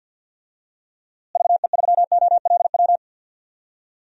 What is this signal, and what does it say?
Morse code, heard as a keyed tone, about 50 wpm, 700 Hz. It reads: VE3OLW
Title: DXCC-100 Call Signs - 6 Characters - Repeated 2x - 50wpm